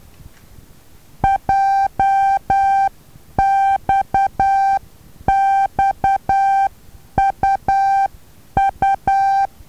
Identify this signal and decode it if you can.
Morse code, heard as a keyed tone, about 10 wpm, 780 Hz. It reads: JXXUU
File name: Taviro HA6VA B